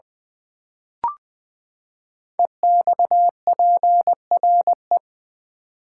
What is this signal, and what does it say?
Morse code, heard as a keyed tone, about 20 wpm, 700 Hz. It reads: EXPRE
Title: Build A Word - 5 Letters At A Time - 10-Letter Words - 20wpm